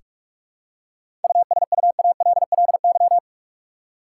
Morse code, heard as a keyed tone, about 46 wpm, 700 Hz. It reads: USUALLY
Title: Seven Letter Words - 45wpm